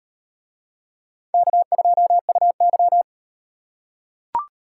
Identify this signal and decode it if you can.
Morse code, heard as a keyed tone, about 36 wpm, 700 Hz. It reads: K2UY